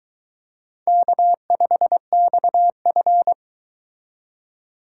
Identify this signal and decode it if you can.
Morse code, heard as a keyed tone, about 22 wpm, 700 Hz. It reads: K5XF